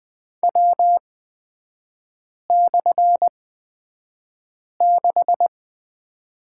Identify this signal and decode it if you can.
Morse code, heard as a keyed tone, about 20 wpm, spaced as for 5 wpm, 690 Hz. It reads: W/6
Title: 229 Lesson 37e - Letter x - Three Characters 20wpm